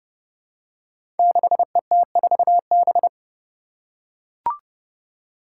Morse code, heard as a keyed tone, about 30 wpm, 700 Hz. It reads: 6ET4B
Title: Five Challenging Letter-Number Combinations 30wpm